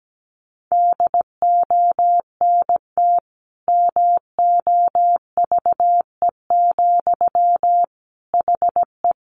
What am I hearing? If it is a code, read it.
Morse code, heard as a keyed tone, about 17 wpm, 700 Hz. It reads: DONT MOVE, HE
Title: Sentences from Top 300 Words 17wpm